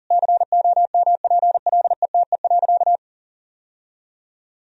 Morse code, heard as a keyed tone, about 37 wpm, 700 Hz. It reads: COMPLETE.
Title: Sentences from Top 1000 Words - Review Part 2 - 40wpm